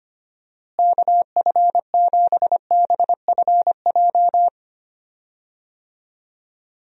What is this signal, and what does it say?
Morse code, heard as a keyed tone, about 25 wpm, 700 Hz. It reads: KF7BFJ
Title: USA Call Signs - 6 Characters - Repeated 2x - 25wpm